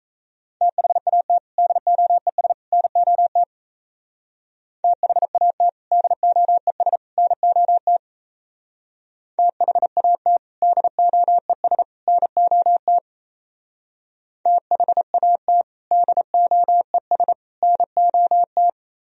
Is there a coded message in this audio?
Yes: Morse code from 0.6 to 18.7 s, heard as a keyed tone, about 36 wpm, 700 Hz. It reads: THAT DOES NOT THAT DOES NOT THAT DOES NOT THAT DOES NOT